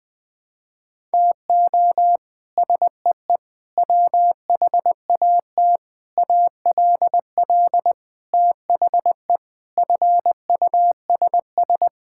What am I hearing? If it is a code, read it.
Morse code, heard as a keyed tone, about 20 wpm, 700 Hz. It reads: TO SEE WHAT ALL THE FUSS
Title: Sets of 8 Words - 20wpm